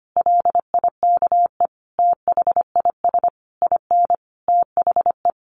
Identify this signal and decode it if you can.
Morse code, heard as a keyed tone, about 25 wpm, 700 Hz. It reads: LIKE THIS IN THE
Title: Sentences from Top 1500 Words 25wpm